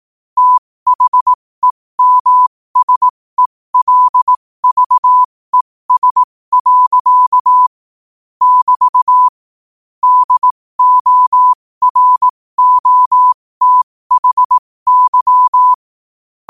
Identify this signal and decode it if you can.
Morse code, heard as a keyed tone, about 18 wpm, spaced as for 15 wpm, 1000 Hz. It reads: THEMSELVES. <BT> DOROTHY
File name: qotdfiles / 15wpm